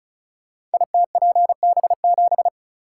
Morse code, heard as a keyed tone, about 36 wpm, 700 Hz. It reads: ITPB7
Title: Code Groups - 5 Letter:Numbers - 35wpm